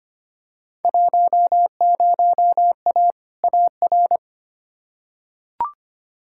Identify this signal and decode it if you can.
Morse code, heard as a keyed tone, about 25 wpm, 700 Hz. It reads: 10A AR
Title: Field Day Exchanges Encore - 25wpm